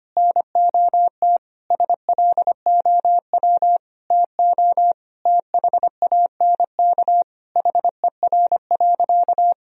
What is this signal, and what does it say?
Morse code, heard as a keyed tone, about 25 wpm, 700 Hz. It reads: NOT SLOW TO THANK HER.
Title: Sentences from Top 1000 Words - Review Part 2 - 25wpm